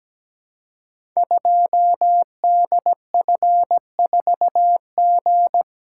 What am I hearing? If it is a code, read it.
Morse code, heard as a keyed tone, about 17 wpm, 700 Hz. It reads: 2DF4G
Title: Code Groups - 5 Letter:Numbers - 17wpm